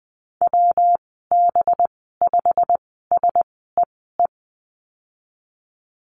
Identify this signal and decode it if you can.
Morse code, heard as a keyed tone, about 20 wpm, spaced as for 15 wpm, 700 Hz. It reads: WB5SEE
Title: 193 Lesson 31g - Letter v - Call Signs 20wpm